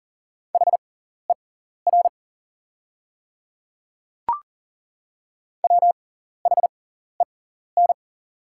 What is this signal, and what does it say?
Morse code, heard as a keyed tone, about 38 wpm, spaced as for 14 wpm, 690 Hz. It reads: HER WHEN
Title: 6x Character Spacing - Top 100 Words - 40wpm